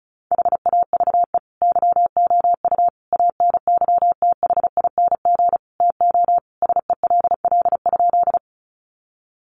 Morse code, heard as a keyed tone, about 34 wpm, 700 Hz. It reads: HAVE YOU ANYTHING TO SELL?
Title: Sentences from Top 1100 Words 35wpm